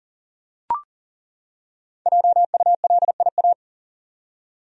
Morse code, heard as a keyed tone, about 40 wpm, 700 Hz. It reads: JULIA
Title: USA Common Names - 40wpm